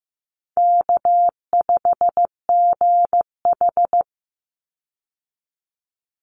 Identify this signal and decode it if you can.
Morse code, heard as a keyed tone, about 15 wpm, 700 Hz. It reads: K5GH
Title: Easy Call Signs - 15wpm